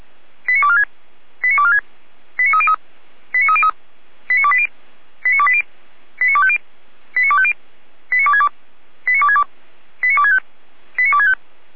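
Five-tone selective call (ZVEI), heard as quick runs of five tones, tones of about 70 ms. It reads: 89246 89246 89303 89303 89280 89280 89250 89250 89262 89262 89265 89265
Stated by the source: ZVEI five-tone selective call received on SDR.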